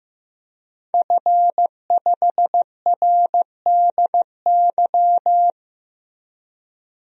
Morse code, heard as a keyed tone, about 15 wpm, 700 Hz. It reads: F5RDY
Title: DXCC-100 Call Signs - 5 Characters - Repeated 3x - 15wpm